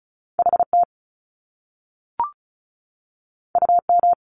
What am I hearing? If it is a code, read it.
Morse code, heard as a keyed tone, about 36 wpm, 700 Hz. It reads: HT UM
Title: Top Two Letter Combinations in Words 35wpm